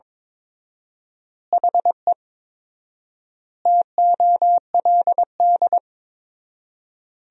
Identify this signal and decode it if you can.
Morse code, heard as a keyed tone, about 22 wpm, 700 Hz. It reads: HE TOLD
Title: 4x Word-Spacing - Sentences from Top 400 Words - 22wpm